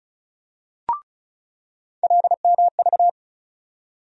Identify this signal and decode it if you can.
Morse code, heard as a keyed tone, about 36 wpm, 700 Hz. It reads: LMV
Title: USA Call Sign Postfixes - 3 Letters 35wpm